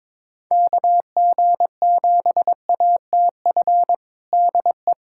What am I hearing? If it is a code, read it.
Morse code, heard as a keyed tone, about 22 wpm, 700 Hz. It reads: KG7ATF DE
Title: USA Callsign DE Callsign - 6 Characters - 22wpm